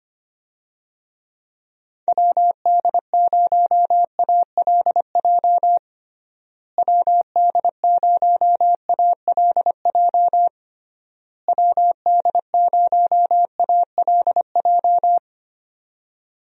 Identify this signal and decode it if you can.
Morse code, heard as a keyed tone, about 25 wpm, 700 Hz. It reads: WD0ALJ WD0ALJ WD0ALJ
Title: USA Call Signs - 6 Characters - Repeated 3x - 25wpm